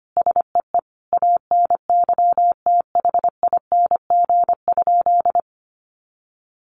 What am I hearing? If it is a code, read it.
Morse code, heard as a keyed tone, about 25 wpm, 700 Hz. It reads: SEE ANYTHING?
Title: Sentences from Top 1000 Words 25wpm